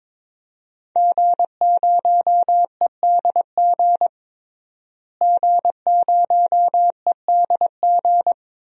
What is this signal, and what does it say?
Morse code, heard as a keyed tone, about 22 wpm, 700 Hz. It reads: G0EDG G0EDG
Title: DXCC-100 Call Signs - 5 Characters - Repeated 2x - 22wpm